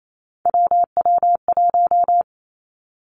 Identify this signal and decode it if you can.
Morse code, heard as a keyed tone, about 28 wpm, 700 Hz. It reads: WW1
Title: Three Challenging Letter-Number Combinations 28wpm